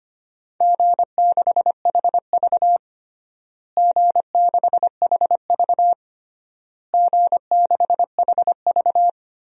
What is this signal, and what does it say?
Morse code, heard as a keyed tone, about 25 wpm, 700 Hz. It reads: G6HV G6HV G6HV
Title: DXCC-100 Call Signs - 4 Characters - Repeated 3x - 25wpm